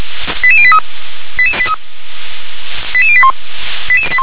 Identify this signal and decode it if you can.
Five-tone selective call (ZVEI), heard as quick runs of five tones, tones of about 70 ms. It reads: 88083 88081